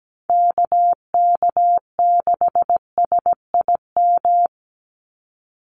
Morse code, heard as a keyed tone, about 17 wpm, 700 Hz. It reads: KK6SIM
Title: USA Call Signs 17wpm